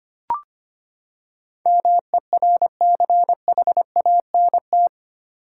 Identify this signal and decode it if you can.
Morse code, heard as a keyed tone, about 25 wpm, 700 Hz. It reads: MERCHANT
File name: Eight Letter Words - 25wpm